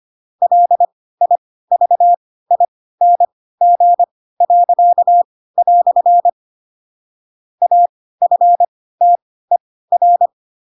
Morse code, heard as a keyed tone, about 25 wpm, spaced as for 16 wpm, 700 Hz. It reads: LIVING." AFTER
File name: Pinocchio / Pinocchio ch3 10 25 ik1hge